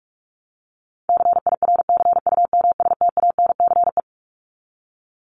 Morse code, heard as a keyed tone, about 44 wpm, 700 Hz. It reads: CIRCUMSTANCE
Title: ES:S Suffix - Top 100 Words with Base Words 45wpm